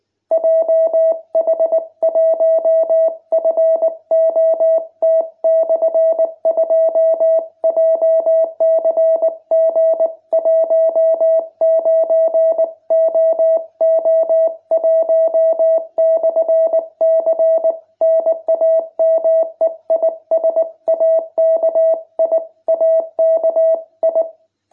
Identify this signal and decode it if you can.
Morse code, heard as a keyed tone, about 19 wpm, 660 Hz. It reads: JH1FOT/2 JCG19OO1/C NAME IS AKI AKI